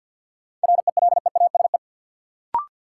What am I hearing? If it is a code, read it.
Morse code, heard as a keyed tone, about 50 wpm, 700 Hz. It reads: RELEASE
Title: Sentences from Top 1100 Words - New Word Drill - 50wpm